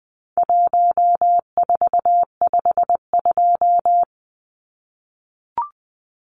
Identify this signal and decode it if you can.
Morse code, heard as a keyed tone, about 20 wpm, 700 Hz. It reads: 1452
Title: Numbers - Contest Style - 4-Digit 20wpm